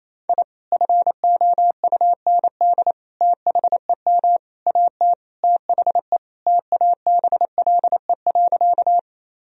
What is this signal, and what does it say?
Morse code, heard as a keyed tone, about 28 wpm, 700 Hz. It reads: I FOUND THEM AT THE TABLE.
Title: Sentences from Top 1000 Words - Review Part 2 - 28wpm